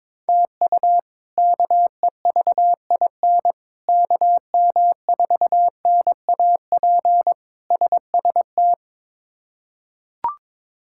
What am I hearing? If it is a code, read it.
Morse code, heard as a keyed tone, about 22 wpm, 700 Hz. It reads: TU KEVIN KM4NAP SST
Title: K1USN SST QSO Practice - 22wpm